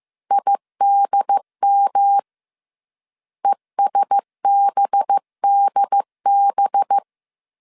Morse code, heard as a keyed tone, about 15 wpm, 800 Hz. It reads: IDM ESBDB